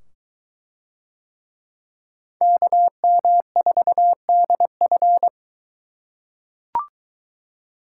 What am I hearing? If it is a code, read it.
Morse code, heard as a keyed tone, about 21 wpm, 700 Hz. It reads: KM4DF